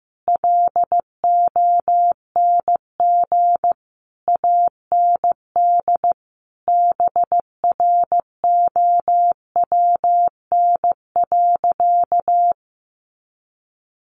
Morse code, heard as a keyed tone, about 15 wpm, 700 Hz. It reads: LONG AND BROWN.
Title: Sentences from Top 1200 Words 15wpm